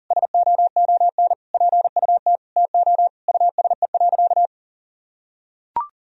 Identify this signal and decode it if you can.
Morse code, heard as a keyed tone, about 40 wpm, 700 Hz. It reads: SOON PUT TO USE.
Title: Sentences from Top 1000 Words - Review Part 3 - 40wpm